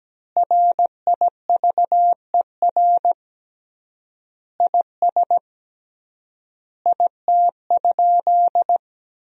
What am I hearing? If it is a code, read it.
Morse code, heard as a keyed tone, about 17 wpm, 700 Hz. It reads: RIVER IS IT?
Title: 3x Word-Spacing - Sentences from Top 300 Words - 17wpm